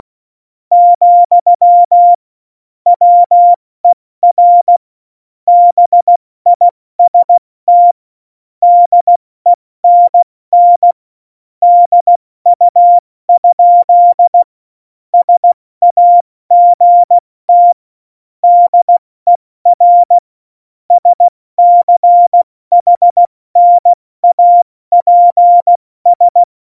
Morse code, heard as a keyed tone, about 16 wpm, 700 Hz. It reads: , WER BIST DENN DU? SAGT DER SCHNAPS
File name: Morsekurs / Witze / Witz 03. November 2020, 70 F80